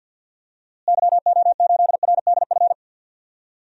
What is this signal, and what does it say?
Morse code, heard as a keyed tone, about 51 wpm, 700 Hz. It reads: YO8RDF
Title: DXCC-100 Call Signs - 6 Characters - Repeated 2x - 50wpm